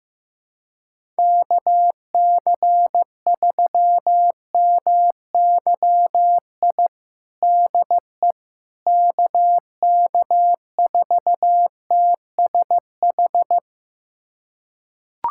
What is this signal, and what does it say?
Morse code, heard as a keyed tone, about 15 wpm, 700 Hz. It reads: KC3MYI DE KK4TSH
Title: USA Callsign DE Callsign - 6 Characters - 15wpm